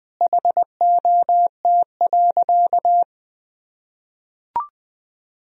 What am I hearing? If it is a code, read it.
Morse code, heard as a keyed tone, about 20 wpm, 700 Hz. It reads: HOT.
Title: Sentences from Top 1000 Words - Review Part 4 - 20wpm